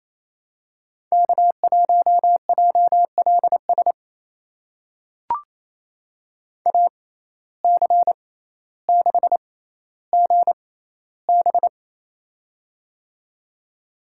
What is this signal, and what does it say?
Morse code, heard as a keyed tone, about 28 wpm, spaced as for 14 wpm, 700 Hz. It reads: K1JLS AC6GB
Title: USA Call Signs - 5 Characters - 6x Character Spacing - 28wpm